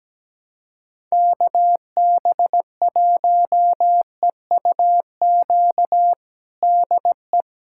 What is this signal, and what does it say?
Morse code, heard as a keyed tone, about 17 wpm, 700 Hz. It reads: KB1EUQ DE
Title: USA Callsign DE Callsign - 6 Characters - 17wpm